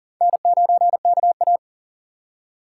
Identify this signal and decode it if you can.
Morse code, heard as a keyed tone, about 40 wpm, 700 Hz. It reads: N9KA
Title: USA Call Signs - 6x Character Spacing - 40wpm